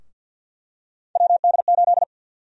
Morse code, heard as a keyed tone, about 50 wpm, 700 Hz. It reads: WD8
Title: USA Call Sign Prefixes - 50wpm